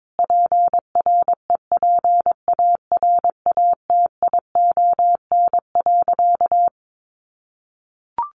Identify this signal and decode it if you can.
Morse code, heard as a keyed tone, about 22 wpm, 700 Hz. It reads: PREPARATION.
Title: Sentences from Top 1400 Words 22wpm